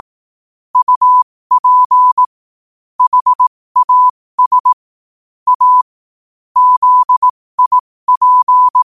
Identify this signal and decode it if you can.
Morse code, heard as a keyed tone, about 18 wpm, spaced as for 15 wpm, 1000 Hz. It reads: UP HAS A ZIP
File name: qotdfiles / 15wpm